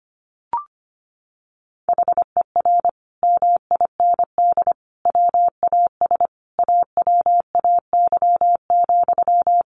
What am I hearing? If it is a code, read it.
Morse code, heard as a keyed tone, about 25 wpm, 700 Hz. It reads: HER MIND WAS AWAY,
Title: Sentences from Top 1500 Words 25wpm